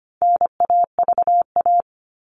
Morse code, heard as a keyed tone, about 25 wpm, 700 Hz. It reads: NAVA
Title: All Cut Numbers - 4-Digit 25wpm